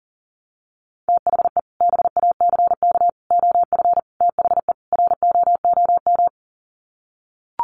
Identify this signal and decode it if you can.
Morse code, heard as a keyed tone, about 40 wpm, 700 Hz. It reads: THE BACK OF THE ROOM
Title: Sets of 5 Words Encore 40wpm